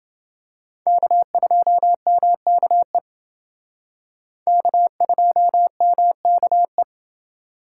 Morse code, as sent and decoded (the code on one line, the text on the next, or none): -.- ..--- -- -.- . / -.- ..--- -- -.- .
K2MKE K2MKE